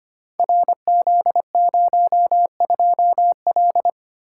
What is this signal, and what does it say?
Morse code, heard as a keyed tone, about 25 wpm, 700 Hz. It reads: RZ02L
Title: Code Groups - 5 Letter:Numbers - 25wpm